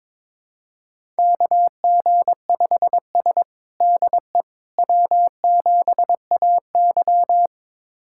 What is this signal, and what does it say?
Morse code, heard as a keyed tone, about 22 wpm, 700 Hz. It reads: KG5S DE W7AY
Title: USA Callsign DE Callsign - 4 Characters - 22wpm